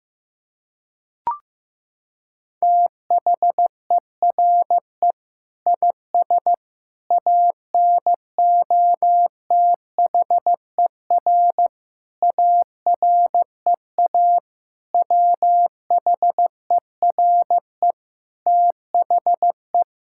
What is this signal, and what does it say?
Morse code, heard as a keyed tone, about 15 wpm, 700 Hz. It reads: THERE IS ANOTHER AREA WHERE THE
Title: Sentences from Top 1000 Words - Review Part 5 - 15wpm